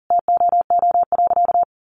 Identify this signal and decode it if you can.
Morse code, heard as a keyed tone, about 40 wpm, 700 Hz. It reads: TOO.
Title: Sentences from Top 1000 Words 40wpm